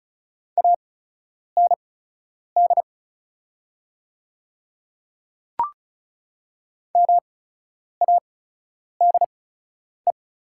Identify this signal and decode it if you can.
Morse code, heard as a keyed tone, about 36 wpm, spaced as for 10 wpm, 700 Hz. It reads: AND MADE
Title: 8x Character Spacing - Top 200 Words - 35wpm